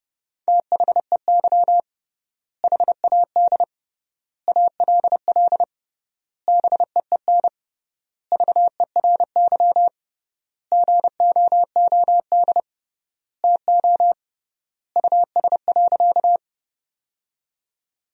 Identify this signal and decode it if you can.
Morse code, heard as a keyed tone, about 32 wpm, 700 Hz. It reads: THEY HAD ALL BEEN VERY GOOD TO US.
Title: 3x Word-Spacing - Sentences from Top 200 Words - 30wpm